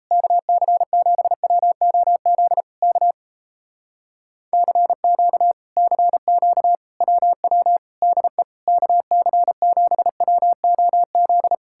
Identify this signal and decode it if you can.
Morse code, heard as a keyed tone, about 38 wpm, 700 Hz. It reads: KC7WOZ K CQ CQ WW DE KC7WOZ